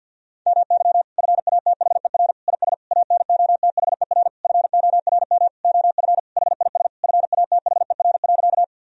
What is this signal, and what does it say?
Morse code, heard as a keyed tone, about 51 wpm, 700 Hz. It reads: MY FATHER IS ANOTHER FORM OF HIS FATHER.
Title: Sentences from Top 1000 Words - Review Part 1 - 50wpm